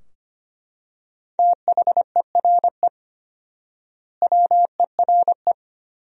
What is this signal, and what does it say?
Morse code, heard as a keyed tone, about 25 wpm, 700 Hz. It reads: THERE WERE
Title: 4x Word-Spacing - Sentences from Top 100 Words - 25wpm